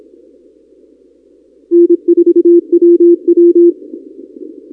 Morse code, heard as a keyed tone, about 26 wpm, 340 Hz. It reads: N4WW
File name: contests / 2019-cw-wpx-kt5j-k5pi- -2330 0-31